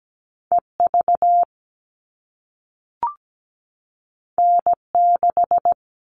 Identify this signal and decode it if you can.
Morse code, heard as a keyed tone, about 17 wpm, 700 Hz. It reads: EV N6
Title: All Cut Numbers - 2-Digit 17wpm